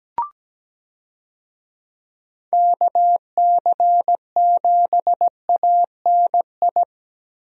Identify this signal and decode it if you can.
Morse code, heard as a keyed tone, about 17 wpm, 700 Hz. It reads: KC7ANI
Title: USA Call Signs - 6 Characters - Repeated 3x - 17wpm